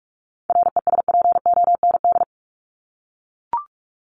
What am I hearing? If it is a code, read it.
Morse code, heard as a keyed tone, about 44 wpm, 700 Hz. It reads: RESPOND
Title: Top 1751-2000 Words 45wpm